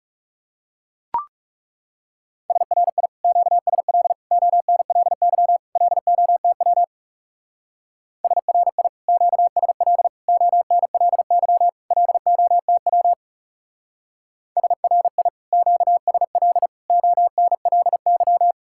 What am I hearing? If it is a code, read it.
Morse code, heard as a keyed tone, about 35 wpm, 700 Hz. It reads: SRI QSL ONLY LOTW SRI QSL ONLY LOTW SRI QSL ONLY
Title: Speed-Racing Common QSO Phrases Encore - 30wpm